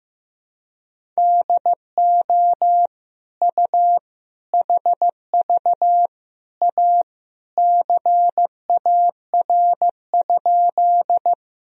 Morse code, heard as a keyed tone, about 15 wpm, 700 Hz. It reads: DO U HV A CAR?
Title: Sentences from Top 300 Words and Abbreviations 15wpm